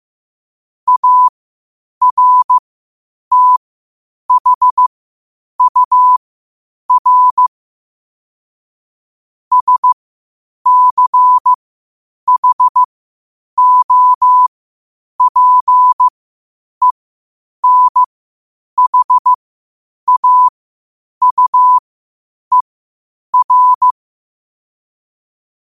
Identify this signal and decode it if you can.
Morse code, heard as a keyed tone, about 15 wpm, spaced as for 8 wpm, 1000 Hz. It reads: ARTHUR SCHOPENHAUER
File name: qotdfiles / 8wpm